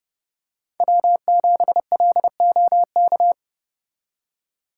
Morse code, heard as a keyed tone, about 30 wpm, 700 Hz. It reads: W7LOK